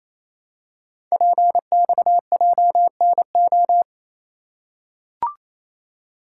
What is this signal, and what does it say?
Morse code, heard as a keyed tone, about 28 wpm, 700 Hz. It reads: PXJNO
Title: Code Groups - 5 Letters - 28wpm